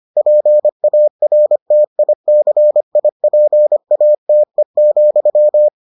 Morse code, heard as a keyed tone, about 25 wpm, 600 Hz. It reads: PARTICIPATE,